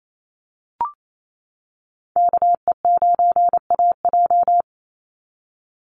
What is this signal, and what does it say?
Morse code, heard as a keyed tone, about 28 wpm, 700 Hz. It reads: KE9AJ
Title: NA SOTA Chaser Callsigns 28wpm